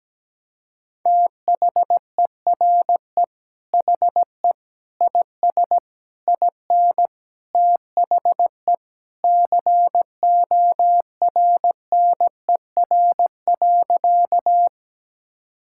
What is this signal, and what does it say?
Morse code, heard as a keyed tone, about 17 wpm, 700 Hz. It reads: THERE HE IS IN THE CORNER.
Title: Sentences from Top 1600 Words 17wpm